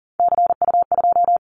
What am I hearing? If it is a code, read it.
Morse code, heard as a keyed tone, about 42 wpm, 700 Hz. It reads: CU2
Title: Callsign Prefixes for Top 100 DXCC - 40wpm